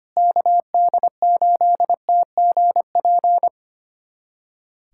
Morse code, heard as a keyed tone, about 25 wpm, 700 Hz. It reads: KD8TGP
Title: USA Callsign DE Callsign - 6 Characters - 25wpm